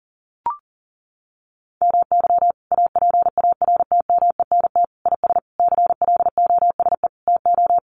Morse code, heard as a keyed tone, about 40 wpm, 700 Hz. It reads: MY APARTMENT IS CLOSE TO
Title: Sentences from Top 900 Words 40wpm